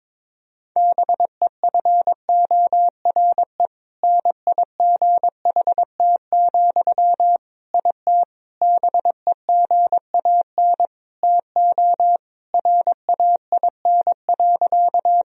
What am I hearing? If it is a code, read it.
Morse code, heard as a keyed tone, about 22 wpm, 700 Hz. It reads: BEFORE NIGHT, IT BEGAN TO RAIN.
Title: Sentences from Top 1000 Words - Review Part 1 - 22wpm